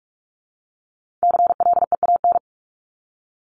Morse code, heard as a keyed tone, about 44 wpm, 700 Hz. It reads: CLEAN
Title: Top 1001-1250 Words 45wpm